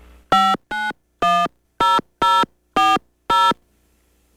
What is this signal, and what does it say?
DTMF, heard as telephone touch tones, tones of about 220 ms. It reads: 5910040